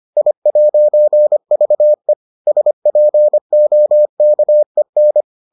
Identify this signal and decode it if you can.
Morse code, heard as a keyed tone, about 25 wpm, 600 Hz. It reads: I'VE SPOKEN